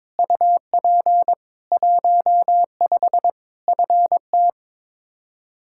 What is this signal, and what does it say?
Morse code, heard as a keyed tone, about 22 wpm, 700 Hz. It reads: UP 15 FT
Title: Practice QSO - Exchange 2 - Info - 22wpm